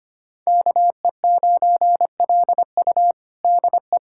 Morse code, heard as a keyed tone, about 25 wpm, 700 Hz. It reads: KE9LU DE
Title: USA Callsign DE Callsign - 5 Characters - 25wpm